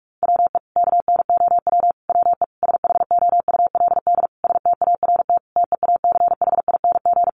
Morse code, heard as a keyed tone, about 44 wpm, 700 Hz. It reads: WE KNOW WE SHOULD START TEACHING
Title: Sentences from Top 1100 Words 45wpm